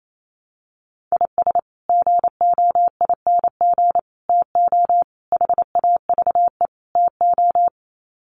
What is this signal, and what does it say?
Morse code, heard as a keyed tone, about 28 wpm, 700 Hz. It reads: IS GOING TO HAVE TO
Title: Sets of 5 Words 28wpm